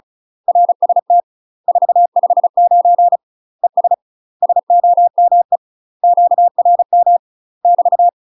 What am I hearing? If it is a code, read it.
Morse code, heard as a keyed tone, about 34 wpm, 700 Hz. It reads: RST 459 ES SOME QRM <BT>
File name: UK MASTER QSOs TRAINING / 35WPM / UK MASTER QSOs 0004